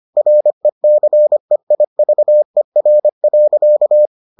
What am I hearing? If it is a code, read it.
Morse code, heard as a keyed tone, about 25 wpm, 600 Hz. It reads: RECEIVER.